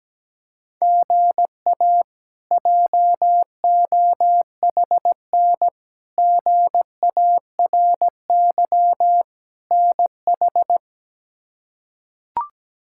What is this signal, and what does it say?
Morse code, heard as a keyed tone, about 17 wpm, 700 Hz. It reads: GA JOHN GARY NH
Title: K1USN SST QSO Practice - 17wpm